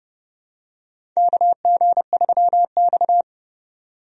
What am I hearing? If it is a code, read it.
Morse code, heard as a keyed tone, about 31 wpm, 700 Hz. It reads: KG3X
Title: USA Call Signs - 4 Characters - 30wpm